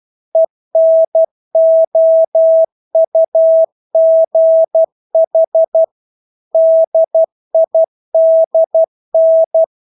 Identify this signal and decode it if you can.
Morse code, heard as a keyed tone, about 12 wpm, 650 Hz. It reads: ENOUGH DIDN
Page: Engelske ord 12 wpm